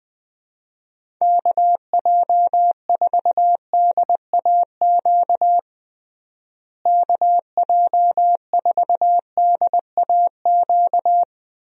Morse code, heard as a keyed tone, about 20 wpm, 700 Hz. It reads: KJ4DAQ KJ4DAQ
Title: USA Call Signs - 6 Characters - Repeated 3x - 20wpm